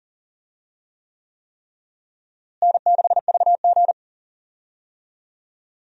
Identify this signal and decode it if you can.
Morse code, heard as a keyed tone, about 40 wpm, 700 Hz. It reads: N6VG
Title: USA Call Signs - 4 Characters - Repeated 3x - 40wpm